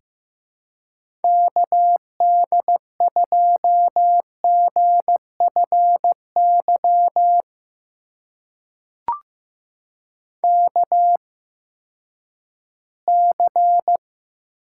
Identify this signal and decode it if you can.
Morse code, heard as a keyed tone, about 15 wpm, spaced as for 10 wpm, 700 Hz. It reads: KD2GFY KC
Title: USA Call Signs - 6 Characters - 8x Character Spacing - 15wpm